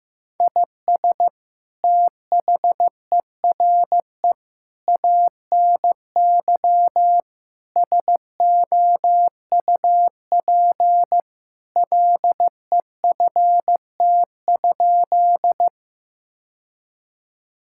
Morse code, heard as a keyed tone, about 15 wpm, 700 Hz. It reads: IS THERE ANY SOUP LEFT?
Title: Sentences from Top 1000 Words - Review Part 6 - 15wpm